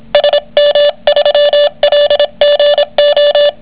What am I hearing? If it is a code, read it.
Morse code, heard as a keyed tone, about 26 wpm, 580 Hz, with noise. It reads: SM3LGO